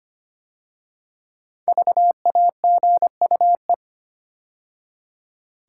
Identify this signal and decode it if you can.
Morse code, heard as a keyed tone, about 25 wpm, 700 Hz. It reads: VAGUE
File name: 5 Letter Words - First 3 Letters Given - 25wpm